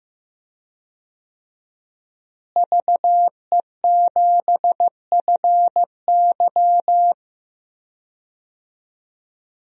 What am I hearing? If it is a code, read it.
Morse code, heard as a keyed tone, about 15 wpm, 700 Hz. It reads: VE7FY
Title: DXCC-100 Call Signs - 5 Characters - Repeated 3x - 15wpm